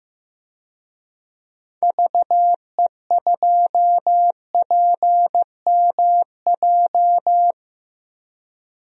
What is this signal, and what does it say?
Morse code, heard as a keyed tone, about 16 wpm, 700 Hz. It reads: VE2PMJ
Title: DXCC-100 Call Signs - 6 Characters - 15wpm